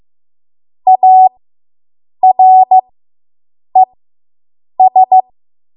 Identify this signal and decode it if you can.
Morse code, heard as a keyed tone, about 15 wpm, spaced as for 7 wpm, 750 Hz. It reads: ARES